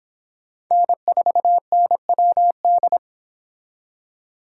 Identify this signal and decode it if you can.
Morse code, heard as a keyed tone, about 25 wpm, 700 Hz. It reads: N4NWD